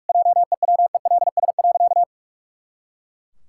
Morse code, heard as a keyed tone, about 41 wpm, 700 Hz. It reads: JEWELS.
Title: Pangram Challenge v2 - 45wpm